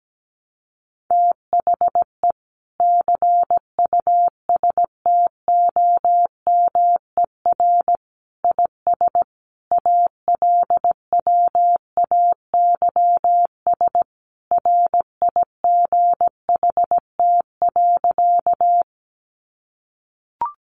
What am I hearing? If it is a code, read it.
Morse code, heard as a keyed tone, about 17 wpm, 700 Hz. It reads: THE CUSTOMER IS ALWAYS RIGHT.
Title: Sentences from Top 1100 Words 17wpm